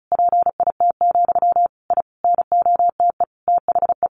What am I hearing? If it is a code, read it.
Morse code, heard as a keyed tone, about 34 wpm, 700 Hz. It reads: PIT, I NOTE THE
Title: A Sound Of Thunder 35wpm